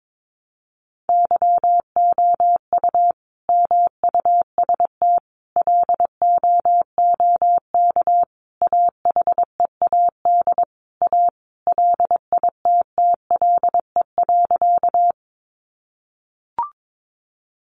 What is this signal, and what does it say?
Morse code, heard as a keyed tone, about 22 wpm, 700 Hz. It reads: YOU MUST LOOK AHEAD A LITTLE.
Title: Sentences from Top 1100 Words 22wpm